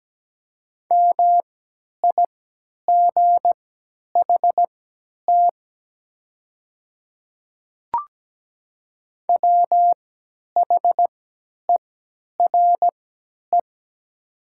Morse code, heard as a keyed tone, about 17 wpm, spaced as for 10 wpm, 700 Hz. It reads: MIGHT WHERE
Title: 3x Character Spacing - Top 300 Words - 17wpm